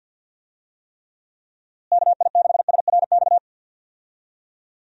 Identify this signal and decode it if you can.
Morse code, heard as a keyed tone, about 52 wpm, 700 Hz. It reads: KI6SRX
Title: USA Call Signs - 6 Characters - Repeated 2x - 50wpm